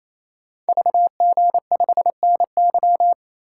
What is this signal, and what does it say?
Morse code, heard as a keyed tone, about 28 wpm, 700 Hz. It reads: VG5NY
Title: Code Groups - 5 Letter:Numbers - 28wpm